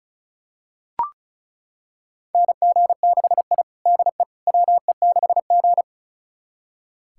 Morse code, heard as a keyed tone, about 34 wpm, 700 Hz. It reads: NG6I DE WE6G
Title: USA Callsign DE Callsign - 4 Characters - 35wpm